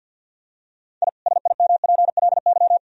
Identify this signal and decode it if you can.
Morse code, heard as a keyed tone, about 50 wpm, 700 Hz. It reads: I SIMPLY
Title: Sentences from Top 1000 Words - Review Part 4 - 50wpm